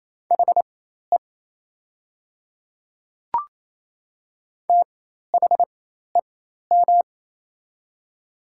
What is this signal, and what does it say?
Morse code, heard as a keyed tone, about 28 wpm, spaced as for 13 wpm, 700 Hz. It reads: HE THEM
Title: 4x Character Spacing - Top 100 Words - 28wpm